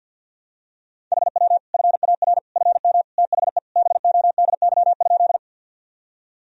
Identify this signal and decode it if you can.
Morse code, heard as a keyed tone, about 51 wpm, 700 Hz. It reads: HW FAR FM THE BODY?
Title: Sentences from Top 400 Words and Abbreviations 50wpm